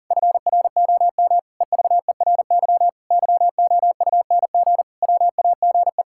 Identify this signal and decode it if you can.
Morse code, heard as a keyed tone, about 40 wpm, 700 Hz. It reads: FROM EVERY YOUNG WAGE
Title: Pangram Challenge v2 - 40wpm